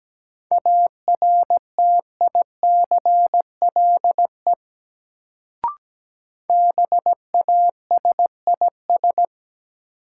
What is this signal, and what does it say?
Morse code, heard as a keyed tone, about 17 wpm, 700 Hz. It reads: ARTICLE BASIS
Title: Sentences from Top 1000 Words - Review Part 5 - New Word Drill - 17wpm